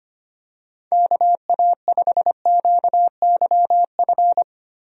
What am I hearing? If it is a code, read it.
Morse code, heard as a keyed tone, about 25 wpm, 700 Hz. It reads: KA5QYF
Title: USA Call Signs - 6 Characters - 25wpm